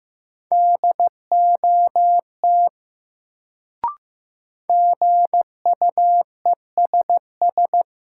Morse code, heard as a keyed tone, about 15 wpm, 700 Hz. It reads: DOT GUESS
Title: Sentences from Top 1500 Words - New Word Drill - 15wpm